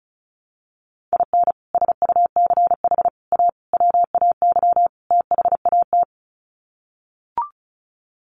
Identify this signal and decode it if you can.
Morse code, heard as a keyed tone, about 34 wpm, 700 Hz. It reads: IN SUCH A WAY THAT
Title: Sets of 5 Words 35wpm